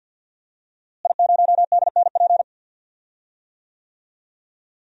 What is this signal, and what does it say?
Morse code, heard as a keyed tone, about 52 wpm, 700 Hz. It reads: I0DNP
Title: DXCC-100 Call Signs - 5 Characters - Repeated 3x - 50wpm